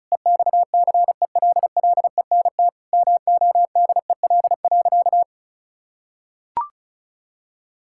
Morse code, heard as a keyed tone, about 36 wpm, 700 Hz. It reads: EXCELLENT MODEL.
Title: Sentences from Top 1000 Words - Review Part 5 - 35wpm